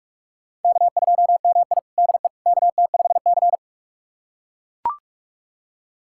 Morse code, heard as a keyed tone, about 41 wpm, 700 Hz. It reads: K2MI DE KT5C
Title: USA Callsign DE Callsign - 4 Characters - 45wpm